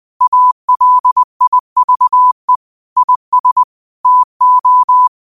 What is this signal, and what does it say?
Morse code, heard as a keyed tone, about 20 wpm, 1000 Hz. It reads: ALIVE IS TO
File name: 2024-09-13-20wpm